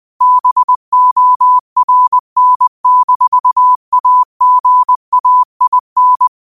Morse code, heard as a keyed tone, about 20 wpm, 1000 Hz. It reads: BORN-AGAIN
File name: qotdfiles / 20wpm